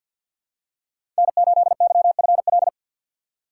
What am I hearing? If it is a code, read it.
Morse code, heard as a keyed tone, about 46 wpm, 700 Hz. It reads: N8YFL
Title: USA Call Signs - 5 Characters - 2x Character Spacing - 50wpm